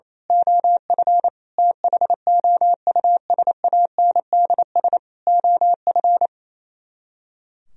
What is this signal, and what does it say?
Morse code, heard as a keyed tone, about 26 wpm, 700 Hz. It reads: OF THOUSANDS OF
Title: Speed-Racing - Sets of 4 Words 28wpm